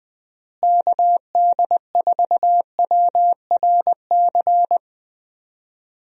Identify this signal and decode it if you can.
Morse code, heard as a keyed tone, about 20 wpm, 700 Hz. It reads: KD4WRC
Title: USA Call Signs - 6 Characters - Repeated 3x - 20wpm